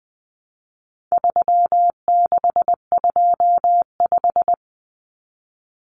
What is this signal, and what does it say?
Morse code, heard as a keyed tone, about 20 wpm, 700 Hz. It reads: 3625
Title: Common Cut Numbers - 4-Digit 20wpm